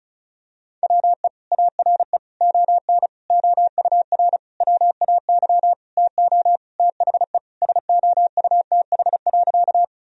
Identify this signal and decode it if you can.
Morse code, heard as a keyed tone, about 34 wpm, 700 Hz. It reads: WE ARE ON OUR WAY TO THE SOUTH.
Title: Sentences from Top 1000 Words - Review Part 2 - 35wpm